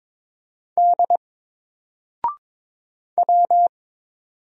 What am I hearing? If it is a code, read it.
Morse code, heard as a keyed tone, about 22 wpm, 700 Hz. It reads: D W
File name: Single-Letter-Number-22wpm